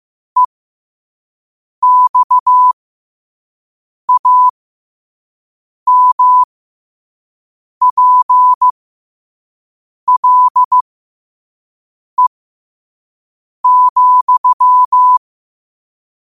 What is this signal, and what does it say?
Morse code, heard as a keyed tone, about 15 wpm, spaced as for 5 wpm, 1000 Hz. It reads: EXAMPLE,
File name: qotdfiles / 5wpm